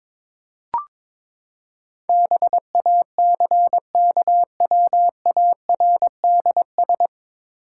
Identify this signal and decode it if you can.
Morse code, heard as a keyed tone, about 22 wpm, 700 Hz. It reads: BACKWARDS
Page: Morse Code Ninja practice 22wpm